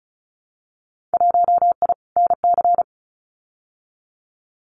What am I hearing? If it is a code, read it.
Morse code, heard as a keyed tone, about 36 wpm, 700 Hz. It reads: 1I NC
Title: Winter Field Day Exchanges 35wpm